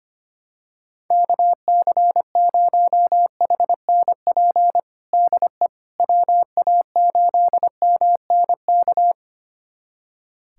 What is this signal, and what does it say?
Morse code, heard as a keyed tone, about 25 wpm, 700 Hz. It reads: KC0HNP DE WA8MNK
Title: USA Callsign DE Callsign 25wpm